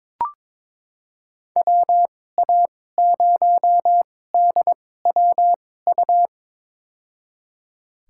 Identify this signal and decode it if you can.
Morse code, heard as a keyed tone, about 22 wpm, spaced as for 16 wpm, 700 Hz. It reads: WA0DWU
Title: USA Call Signs - 6 Characters - 2x Character Spacing - 22wpm